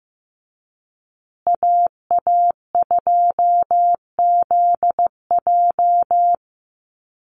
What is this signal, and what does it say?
Morse code, heard as a keyed tone, about 15 wpm, 700 Hz. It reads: AA2ZJ
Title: USA Call Signs Repeated 2x 15wpm